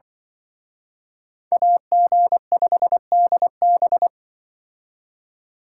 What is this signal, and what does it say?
Morse code, heard as a keyed tone, about 24 wpm, 700 Hz. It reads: AG5DB
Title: Speed Racing - USA 2x2 Call Signs 20wpm